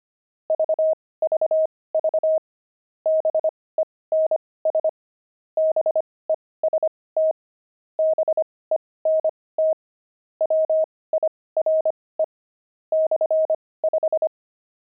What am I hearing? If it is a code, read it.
Morse code, heard as a keyed tone, about 25 wpm, spaced as for 18 wpm, 630 Hz. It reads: VVV BENS BEST BENT WIRE /5
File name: pangram-fs-18-cs-25-wire